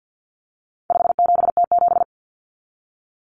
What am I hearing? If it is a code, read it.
Morse code, heard as a keyed tone, about 50 wpm, 700 Hz. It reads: 57T7
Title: Common Cut Numbers - 4-Digit 50wpm